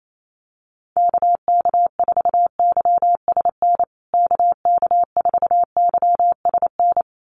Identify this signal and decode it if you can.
Morse code, heard as a keyed tone, about 28 wpm, 700 Hz. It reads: KK4YSN KK4YSN
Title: USA Call Signs Repeated 2x 28wpm